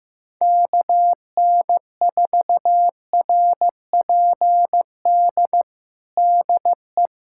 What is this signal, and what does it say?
Morse code, heard as a keyed tone, about 15 wpm, 700 Hz. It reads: KN4RPD DE
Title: USA Callsign DE Callsign - 6 Characters - 15wpm